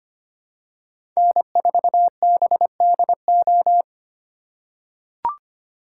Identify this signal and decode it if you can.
Morse code, heard as a keyed tone, about 25 wpm, 700 Hz. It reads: N4BDO
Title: USA Call Signs - 5 Characters - 3x Character Spacing - 25wpm